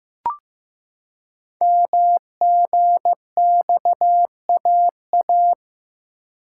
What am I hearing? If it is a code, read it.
Morse code, heard as a keyed tone, about 15 wpm, 700 Hz. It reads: MGXAA
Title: Code Groups - 5 Letters - 15wpm